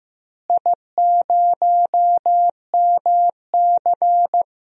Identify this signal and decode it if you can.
Morse code, heard as a keyed tone, about 15 wpm, 700 Hz. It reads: I0MC
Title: DXCC-100 Call Signs - 4 Characters - Repeated 3x - 15wpm